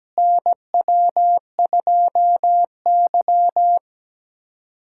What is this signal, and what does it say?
Morse code, heard as a keyed tone, about 17 wpm, 700 Hz. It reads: NW2Y
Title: USA Callsign DE Callsign - 4 Characters - 17wpm